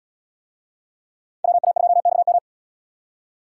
Morse code, heard as a keyed tone, about 66 wpm, 700 Hz. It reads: WA2XM